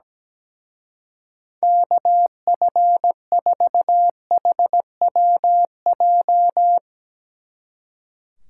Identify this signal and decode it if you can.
Morse code, heard as a keyed tone, about 17 wpm, 700 Hz. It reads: KF4HWJ
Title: Speed Racing - USA 2x3 Call Signs 17wpm